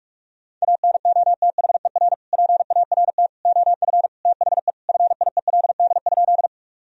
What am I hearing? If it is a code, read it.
Morse code, heard as a keyed tone, about 46 wpm, 700 Hz. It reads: ANOTHER PART OF THE FIELD?
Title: Sentences from Top 1000 Words - Review Part 2 - 45wpm